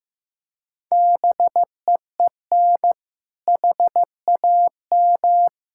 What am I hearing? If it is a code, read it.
Morse code, heard as a keyed tone, about 15 wpm, 700 Hz. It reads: BEEN HAM
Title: Practice QSO - Exchange 3 - More Info - 15wpm